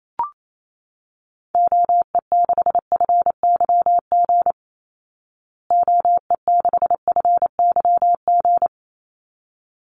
Morse code, heard as a keyed tone, about 28 wpm, 700 Hz. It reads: OE6FYG OE6FYG
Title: Call Signs Repeated 3x - 28wpm